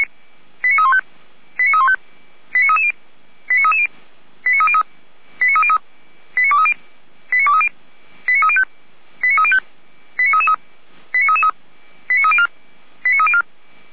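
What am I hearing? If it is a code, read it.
Five-tone selective call (ZVEI), heard as quick runs of five tones, tones of about 70 ms. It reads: 89315 89315 89330 89330 89303 89303 89230 89230 89305 89305 89333 89333 89304 89304
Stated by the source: ZVEI five-tone selective call received on an SDR.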